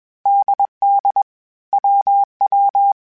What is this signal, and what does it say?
Morse code, heard as a keyed tone, about 21 wpm, 800 Hz. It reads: DD WW
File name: Groups@20 / Groups@20 mp3s / AWUNDG@20